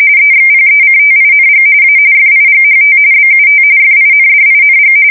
RTTY, about 45.45 baud. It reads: EXPECTING A FIGHT BUT FINDING
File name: fsk170 / battleshipnj